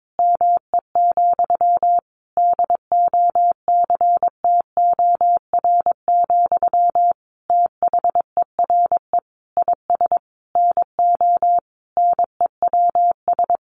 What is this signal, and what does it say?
Morse code, heard as a keyed tone, about 22 wpm, 700 Hz. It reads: ME, DOCTOR, THERE IS NO NEWS
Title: Sentences from Top 1500 Words 22wpm